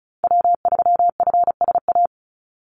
Morse code, heard as a keyed tone, about 34 wpm, 700 Hz. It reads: W3FSA
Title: Call Signs Repeated 3x - 35wpm